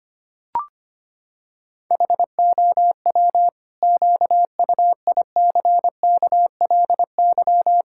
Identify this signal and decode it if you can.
Morse code, heard as a keyed tone, about 25 wpm, 700 Hz. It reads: HOW QUICKLY
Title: Pangram Challenge v2 - 25wpm